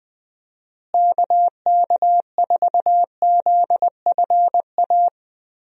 Morse code, heard as a keyed tone, about 20 wpm, 700 Hz. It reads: KK4ZFA
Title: USA Call Signs - 6 Characters - Repeated 3x - 20wpm